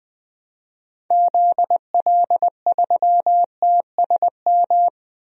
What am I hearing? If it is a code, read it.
Morse code, heard as a keyed tone, about 21 wpm, 700 Hz. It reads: ZL3TSM
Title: DXCC-100 Call Signs - 6 Characters - 20wpm